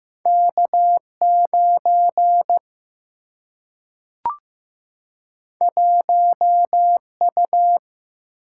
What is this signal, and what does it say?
Morse code, heard as a keyed tone, about 15 wpm, 700 Hz. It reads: K 9 1 U
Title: Single-Letter Single-Number 15wpm